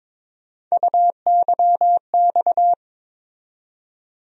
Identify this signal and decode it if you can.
Morse code, heard as a keyed tone, about 21 wpm, 700 Hz. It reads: UYX
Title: USA Call Sign Postfixes - 3 Letters 22wpm